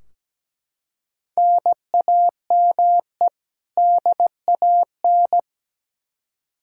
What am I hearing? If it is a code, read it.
Morse code, heard as a keyed tone, about 17 wpm, 700 Hz. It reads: NAME DAN
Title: Practice QSO - Exchange 1 - Beginning - 17wpm